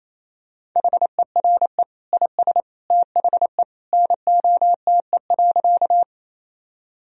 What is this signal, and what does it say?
Morse code, heard as a keyed tone, about 28 wpm, 700 Hz. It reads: HERE IS THE NOTE.
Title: Sentences from Top 1000 Words - Review Part 2 - 28wpm